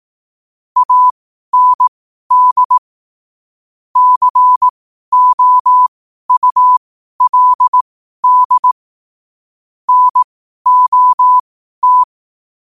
Morse code, heard as a keyed tone, about 18 wpm, spaced as for 12 wpm, 1000 Hz. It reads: AND COULD NOT
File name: qotdfiles / 12wpm